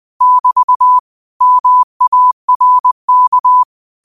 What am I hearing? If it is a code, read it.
Morse code, heard as a keyed tone, about 20 wpm, 1000 Hz. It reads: <BT> MARK
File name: 20wpm